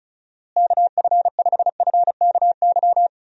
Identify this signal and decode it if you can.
Morse code, heard as a keyed tone, about 36 wpm, 700 Hz. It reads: KF5FKY
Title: USA Call Signs - 6 Characters - Repeated 2x - 35wpm